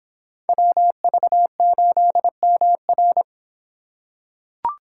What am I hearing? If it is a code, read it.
Morse code, heard as a keyed tone, about 23 wpm, 700 Hz. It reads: WV8MR